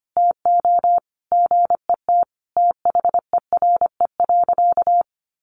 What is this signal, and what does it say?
Morse code, heard as a keyed tone, about 25 wpm, 700 Hz. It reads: TO GET THERE.
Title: Sentences from Top 1500 Words 25wpm